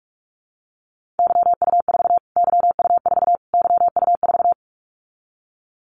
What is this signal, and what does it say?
Morse code, heard as a keyed tone, about 44 wpm, 700 Hz. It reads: YU4 YU4 YU4
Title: Callsign Prefixes for Top 25 DXCC - Repeated 3x - 45wpm